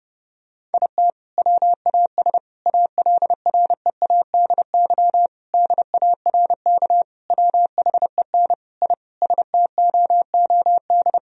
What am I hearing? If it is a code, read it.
Morse code, heard as a keyed tone, about 30 wpm, 700 Hz. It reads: IT WAS ALREADY DARK WHEN I STOOD
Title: Sentences from Top 1000 Words - Review Part 2 - 30wpm